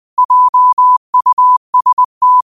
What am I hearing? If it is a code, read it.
Morse code, heard as a keyed tone, about 20 wpm, 1000 Hz. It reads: JUST
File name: qotdfiles / 20wpm